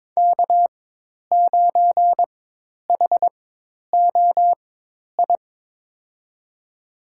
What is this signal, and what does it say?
Morse code, heard as a keyed tone, about 22 wpm, spaced as for 10 wpm, 700 Hz. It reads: K9HOI
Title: USA Call Signs - 5 Characters - 4x Character Spacing - 22wpm